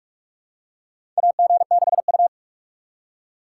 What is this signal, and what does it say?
Morse code, heard as a keyed tone, about 42 wpm, 700 Hz. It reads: AG6U
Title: USA Call Signs - 4 Characters - 4x Character Spacing - 45wpm